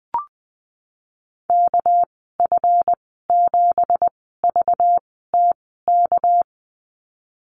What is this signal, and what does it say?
Morse code, heard as a keyed tone, about 20 wpm, spaced as for 15 wpm, 700 Hz. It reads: KF7VTK
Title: 231 Lesson 37g - Letter x - Call Signs 20wpm